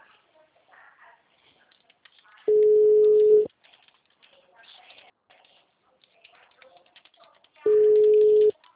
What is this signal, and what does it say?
Ringback tone (Europe, 425 Hz).
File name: out- -2018- - - .46560